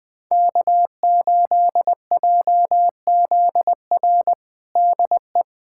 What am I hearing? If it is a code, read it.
Morse code, heard as a keyed tone, about 20 wpm, 700 Hz. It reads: K8JZR DE
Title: USA Callsign DE Callsign - 5 Characters - 20wpm